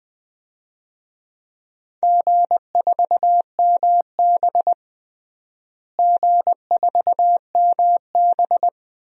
Morse code, heard as a keyed tone, about 20 wpm, 700 Hz. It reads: G4MB G4MB
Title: DXCC-100 Call Signs - 4 Characters - Repeated 2x - 20wpm